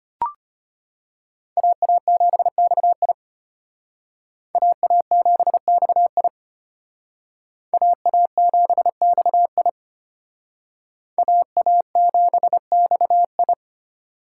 Morse code, heard as a keyed tone, about 32 wpm, 700 Hz. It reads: AA7XI AA7XI AA7XI AA7XI